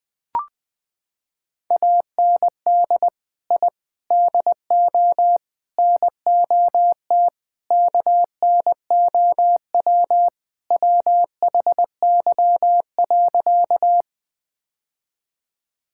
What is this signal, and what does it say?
Morse code, heard as a keyed tone, about 19 wpm, 700 Hz. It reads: AND I DO NOT KNOW WHY.
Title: Sentences with 5-Word Phrases - 20wpm